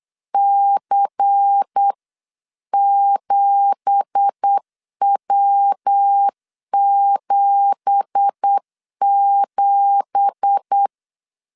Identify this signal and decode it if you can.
Morse code, heard as a keyed tone, about 8 wpm, 800 Hz. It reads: C7W77
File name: LEZIONI / 9 2867 8wpm 3440